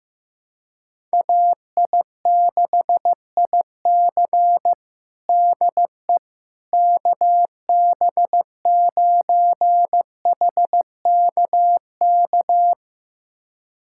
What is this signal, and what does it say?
Morse code, heard as a keyed tone, about 15 wpm, 700 Hz. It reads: AI6IC DE KB9HKK
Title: USA Callsign DE Callsign 15wpm